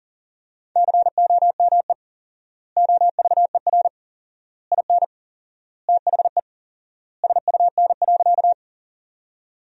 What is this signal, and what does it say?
Morse code, heard as a keyed tone, about 40 wpm, 700 Hz. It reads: COME OVER IN THE SUN.
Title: 4x Word-Spacing - Sentences from Top 300 Words - 40wpm